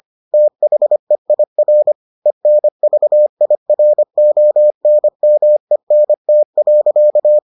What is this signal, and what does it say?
Morse code, heard as a keyed tone, about 25 wpm, 600 Hz. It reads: THEIR ENVIRONMENT.